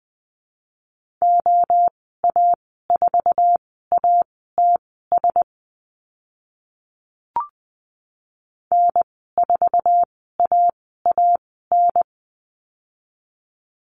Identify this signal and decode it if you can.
Morse code, heard as a keyed tone, about 20 wpm, spaced as for 15 wpm, 700 Hz. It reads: OA4ATS N4AAN
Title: 069 Lesson 12g - Letter d - Call Signs 20wpm